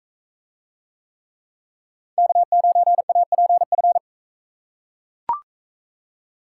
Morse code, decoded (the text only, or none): K9APF